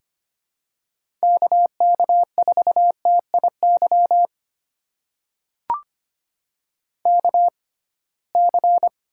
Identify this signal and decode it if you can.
Morse code, heard as a keyed tone, about 25 wpm, spaced as for 19 wpm, 700 Hz. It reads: KK4TIY KC
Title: USA Call Signs - 6 Characters - 6x Character Spacing - 25wpm